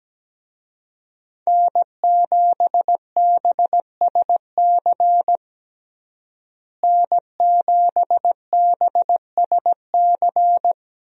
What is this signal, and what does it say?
Morse code, heard as a keyed tone, about 17 wpm, 700 Hz. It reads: N7BSC N7BSC
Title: USA Call Signs - 5 Characters - Repeated 2x - 17wpm